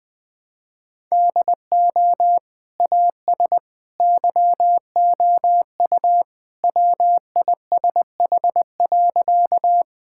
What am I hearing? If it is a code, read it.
Morse code, heard as a keyed tone, about 20 wpm, 700 Hz. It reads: DO AS YOU WISH.
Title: Sentences with 2-Word Phrases - 20wpm